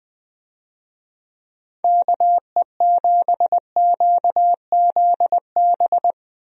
Morse code, heard as a keyed tone, about 20 wpm, 700 Hz. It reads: KE7QZB
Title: USA Call Signs - 6 Characters - 20wpm